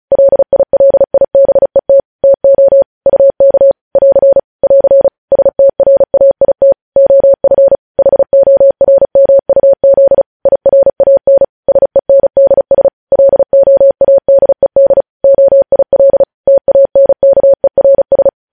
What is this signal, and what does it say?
Morse code, heard as a keyed tone, about 36 wpm, 550 Hz. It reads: LILIBET TO UK <AR> <AR> STRAIT OF HORMUZ IRAN SENDS LOADED OIL TANKERS
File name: podcast / News headlines-1x-35x35